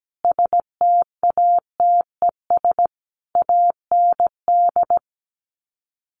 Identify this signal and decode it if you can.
Morse code, heard as a keyed tone, about 17 wpm, 700 Hz. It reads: STATES AND
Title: Sets of 5 Words 17wpm